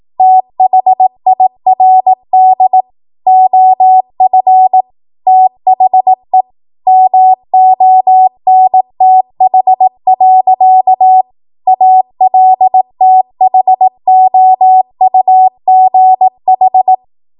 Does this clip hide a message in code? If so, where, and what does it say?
Yes: Morse code from 0.2 to 16.9 s, heard as a keyed tone, about 18 wpm, 750 Hz. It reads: THIRD OF THE MONTH. ALTHOUGH